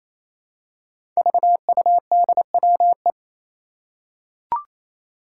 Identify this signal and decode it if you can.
Morse code, heard as a keyed tone, about 27 wpm, 700 Hz. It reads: VUDWE
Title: Five Challenging Letter-Number Combinations 28wpm